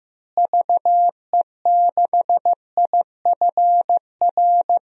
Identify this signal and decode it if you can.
Morse code, heard as a keyed tone, about 15 wpm, 700 Hz. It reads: VE6IFR
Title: DXCC-100 Call Signs - 6 Characters - Repeated 3x - 15wpm